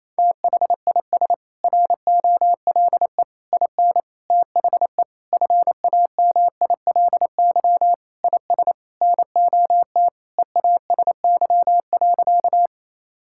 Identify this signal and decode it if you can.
Morse code, heard as a keyed tone, about 27 wpm, 700 Hz. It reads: THIS ROLE IN THE FAMILY IS NOT EASY.
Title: Sentences from Top 1000 Words - Review Part 6 - 28wpm